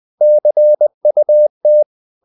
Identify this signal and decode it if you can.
Morse code, heard as a keyed tone, about 20 wpm, 600 Hz. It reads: CUT
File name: the old man and the sea part 2 20wpm